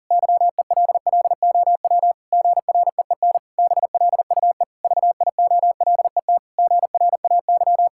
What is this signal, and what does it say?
Morse code, heard as a keyed tone, about 40 wpm, 700 Hz. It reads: YELLOW GREEN BLUE VIOLET GRAY
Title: Pangram Challenge v2 - 40wpm